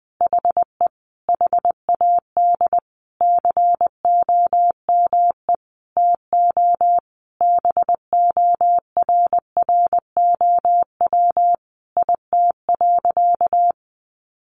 Morse code, heard as a keyed tone, about 20 wpm, 700 Hz. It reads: HE HAD COME TO BORROW IT.
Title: Sentences from Top 1500 Words 20wpm